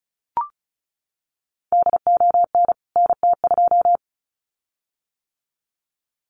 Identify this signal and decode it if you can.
Morse code, heard as a keyed tone, about 34 wpm, 700 Hz. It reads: DON NT2
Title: CWT Exchange 35wpm